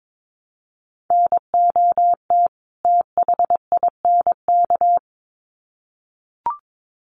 Morse code, heard as a keyed tone, about 22 wpm, 700 Hz. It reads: NOT THINK
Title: Sets of 2 Words 22wpm